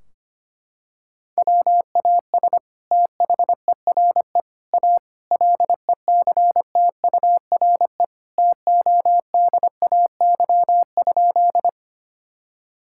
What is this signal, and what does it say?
Morse code, heard as a keyed tone, about 25 wpm, 700 Hz. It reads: WAS THERE A LECTURE TODAY?
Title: Sentences from Top 1600 Words 25wpm